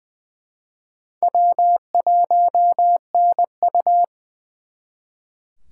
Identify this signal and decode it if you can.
Morse code, heard as a keyed tone, about 25 wpm, 700 Hz. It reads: W1NU
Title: Speed Racing - USA 1x2 Call Signs 20wpm